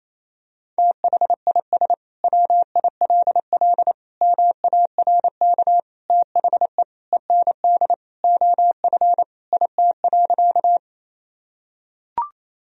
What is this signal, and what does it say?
Morse code, heard as a keyed tone, about 27 wpm, 700 Hz. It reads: THIS WILL MARK THE END OF IT.
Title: Sentences from Top 1000 Words - Review Part 1 - 28wpm